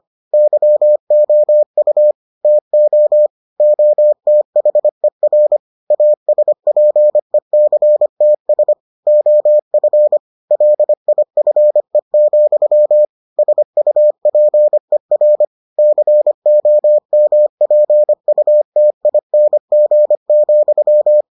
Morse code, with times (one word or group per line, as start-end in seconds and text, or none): 0.3-2.1 YOU
2.4-3.3 TO
3.6-5.6 OTHER
5.9-8.7 ASPECTS
9.1-10.2 OF
10.5-13.1 LIFE,
13.4-15.4 SUPER
15.8-21.2 COMPUTING,